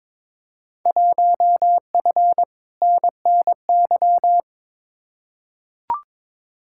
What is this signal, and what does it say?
Morse code, heard as a keyed tone, about 21 wpm, 700 Hz. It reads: 1F NNY
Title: Field Day Exchanges - 22wpm